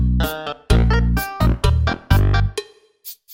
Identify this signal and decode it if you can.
There is no code.